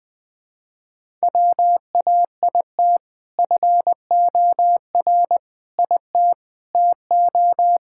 Morse code, heard as a keyed tone, about 21 wpm, 700 Hz. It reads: WAIT FOR IT TO
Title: Sentences with 2-Word Phrases - 20wpm